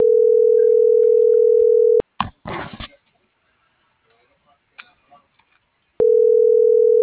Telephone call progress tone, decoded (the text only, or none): ringback tone